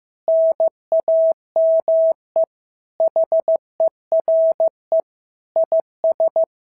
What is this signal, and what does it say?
Morse code, heard as a keyed tone, about 14 wpm, 650 Hz. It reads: NAME HERE IS
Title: SOTA QSO Practice 15wpm